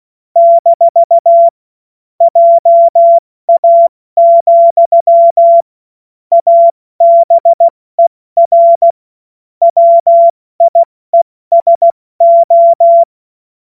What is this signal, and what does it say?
Morse code, heard as a keyed tone, about 16 wpm, 680 Hz. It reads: - JA, ABER WIESO
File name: Witz-09-September-2025-70-F80